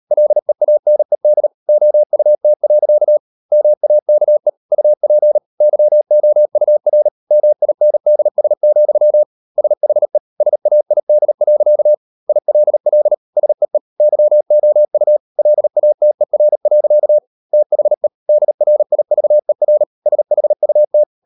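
Morse code, heard as a keyed tone, about 38 wpm, 600 Hz. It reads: LEANED OUT. MAKE UP YOUR MINDS, SHE SAID. ILL SEE YOU LATER. THE DRIVER SHUT
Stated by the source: Morse code practice at 38 WPM, sending Ken Follett's novel Never in English, from CW med Gnister.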